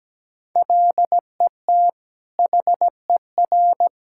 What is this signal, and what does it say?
Morse code, heard as a keyed tone, about 17 wpm, 700 Hz. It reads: LET HER
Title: Sentences from Top 1000 Words - Review Part 6 - 17wpm